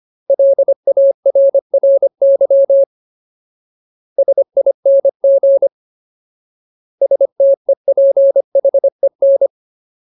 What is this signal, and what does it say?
Morse code, heard as a keyed tone, about 25 wpm, 550 Hz. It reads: LARRY SING STEPHEN
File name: Ultimate Word List / 25WPM Ultimate